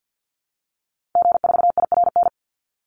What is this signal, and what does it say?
Morse code, heard as a keyed tone, about 50 wpm, 700 Hz. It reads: G4IRN
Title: Call Signs - 50wpm